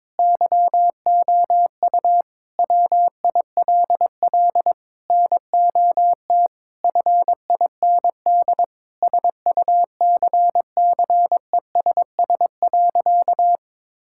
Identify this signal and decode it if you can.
Morse code, heard as a keyed tone, about 22 wpm, 700 Hz. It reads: YOU WILL NOT FIND SUCCESS.
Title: Sentences from Top 1000 Words - Review Part 5 - 22wpm